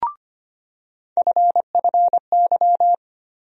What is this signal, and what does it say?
Morse code, heard as a keyed tone, about 25 wpm, 700 Hz. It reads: FFY
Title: USA Call Sign Postfixes - 3 Letters 25wpm